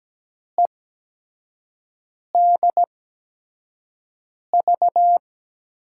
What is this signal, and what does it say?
Morse code, heard as a keyed tone, about 17 wpm, spaced as for 5 wpm, 700 Hz. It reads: EDV
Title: USA Call Signs - 6 Characters - 8x Character Spacing - 17wpm